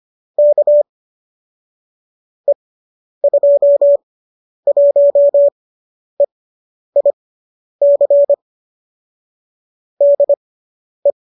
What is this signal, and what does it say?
Morse code, heard as a keyed tone, about 25 wpm, spaced as for 10 wpm, 570 Hz. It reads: K E21EIC DE
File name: qso005 10